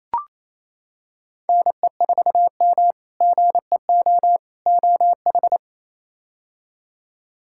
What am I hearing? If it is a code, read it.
Morse code, heard as a keyed tone, about 28 wpm, 700 Hz. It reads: NE4M GEO OH
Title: K1USN SST QSO Practice - 28wpm